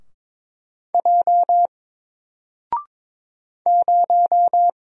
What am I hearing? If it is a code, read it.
Morse code, heard as a keyed tone, about 22 wpm, 700 Hz. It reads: J 0
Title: Single Letter-Number Alt 22wpm